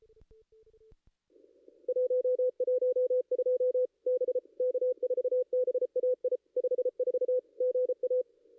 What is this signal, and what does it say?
Morse code, heard as a keyed tone, about 33 wpm, 500 Hz. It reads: 112 B K4BAI 54 GA
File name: contests / ss-cw-k5tr-op-n5rz -2200